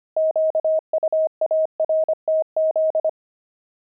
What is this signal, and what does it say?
Morse code, heard as a keyed tone, about 25 wpm, 630 Hz. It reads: QUARTZ
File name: pangram-fs-25-cs-25-jackdaws